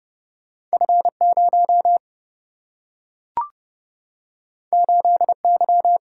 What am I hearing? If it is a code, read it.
Morse code, heard as a keyed tone, about 29 wpm, 700 Hz. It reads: F0 8Y
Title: Single-Letter Single-Number 30wpm